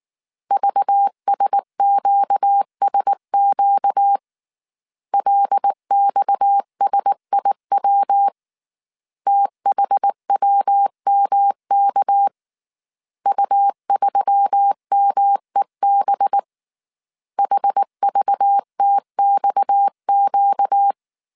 Morse code, heard as a keyed tone, about 19 wpm, 800 Hz. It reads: VSQSQ LXSIW THWMK U3MEB HVTXQ